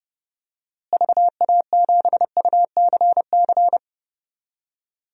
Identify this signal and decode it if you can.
Morse code, heard as a keyed tone, about 32 wpm, 700 Hz. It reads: VA7UCC
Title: DXCC-100 Call Signs - 6 Characters - Repeated 2x - 30wpm